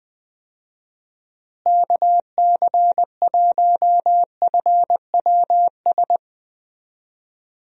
Morse code, heard as a keyed tone, about 20 wpm, 700 Hz. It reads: KC1FWS
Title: USA Call Signs - 6 Characters - Repeated 2x - 20wpm